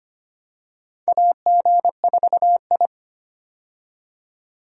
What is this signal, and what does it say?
Morse code, heard as a keyed tone, about 25 wpm, 700 Hz. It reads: AG4I
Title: USA Call Signs - 4 Characters - 25wpm